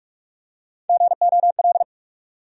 Morse code, heard as a keyed tone, about 45 wpm, 700 Hz. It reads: GOL
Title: USA Call Sign Postfixes - 3 Letters 45wpm